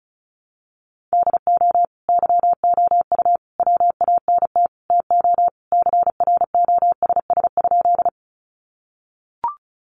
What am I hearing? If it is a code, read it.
Morse code, heard as a keyed tone, about 34 wpm, 700 Hz. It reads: DO YOU WANT TO CROSS?
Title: Sentences from Top 300 Words 35wpm